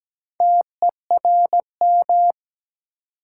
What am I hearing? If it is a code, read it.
Morse code, heard as a keyed tone, about 17 wpm, 700 Hz. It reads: TERM
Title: Sentences from Top 900 Words - New Word Drill - 17wpm